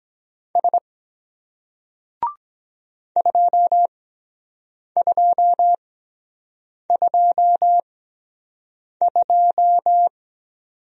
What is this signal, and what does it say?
Morse code, heard as a keyed tone, about 23 wpm, spaced as for 7 wpm, 700 Hz. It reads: S 2222